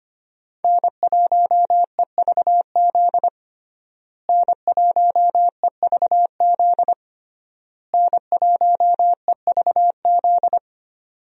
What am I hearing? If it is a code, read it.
Morse code, heard as a keyed tone, about 25 wpm, 700 Hz. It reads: N1EVZ N1EVZ N1EVZ
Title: USA Call Signs - 5 Characters - Repeated 3x - 25wpm